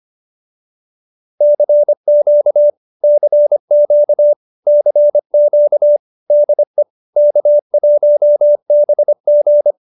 Morse code, heard as a keyed tone, about 25 wpm, 590 Hz. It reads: CQ CQ CQ DE K1BG
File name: qso201b 25